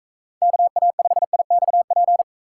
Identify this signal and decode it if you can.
Morse code, heard as a keyed tone, about 43 wpm, 700 Hz. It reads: KA5IXP